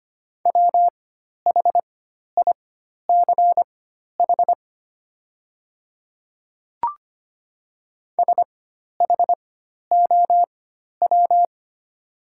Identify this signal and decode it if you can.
Morse code, heard as a keyed tone, about 25 wpm, spaced as for 12 wpm, 700 Hz. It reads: WHICH SHOW
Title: 4x Character Spacing - Top 200 Words - 25wpm